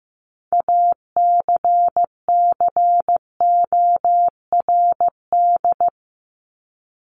Morse code, heard as a keyed tone, about 15 wpm, 700 Hz. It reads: ACCORD
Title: ING Suffix - Top 100 Words with Base Words 15wpm